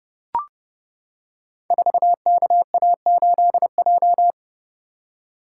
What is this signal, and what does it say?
Morse code, heard as a keyed tone, about 30 wpm, 700 Hz. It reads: 4KA8J
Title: Code Groups - 5 Letter:Numbers - 30wpm